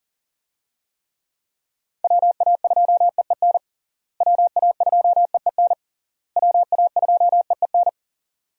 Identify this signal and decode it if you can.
Morse code, heard as a keyed tone, about 40 wpm, 700 Hz. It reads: WA2EEN WA2EEN WA2EEN
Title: USA Call Signs - 6 Characters - Repeated 3x - 40wpm